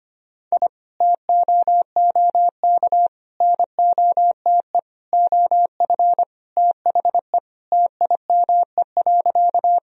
Morse code, heard as a keyed tone, about 25 wpm, 700 Hz. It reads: I TOOK NOTE OF THE TIME.
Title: Sentences from Top 1000 Words - Review Part 2 - 25wpm